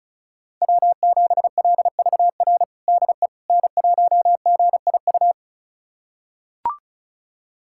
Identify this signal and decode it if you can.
Morse code, heard as a keyed tone, about 34 wpm, 690 Hz. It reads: W7LVR DE N1GIU
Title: USA Callsign DE Callsign - 5 Characters - 35wpm